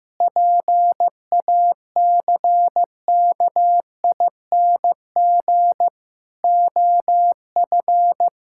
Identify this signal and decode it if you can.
Morse code, heard as a keyed tone, about 15 wpm, 700 Hz. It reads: PACKING OF
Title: Pangram Challenge v2 - 15wpm